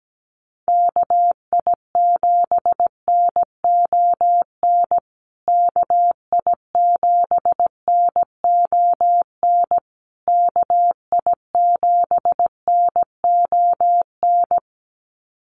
Morse code, heard as a keyed tone, about 17 wpm, 700 Hz. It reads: KI7NON KI7NON KI7NON
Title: USA Call Signs Repeated 3x 17wpm